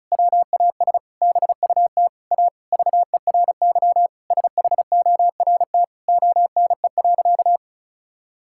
Morse code, heard as a keyed tone, about 34 wpm, 700 Hz. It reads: WAS BUT A VERY SHORT ONE.
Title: Sentences from Top 1600 Words 35wpm